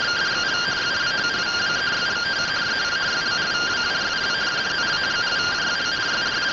Two-tone teleprinter signal, about 170 Hz shift.
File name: findings / 100 surview 0.gen-36 rtty rx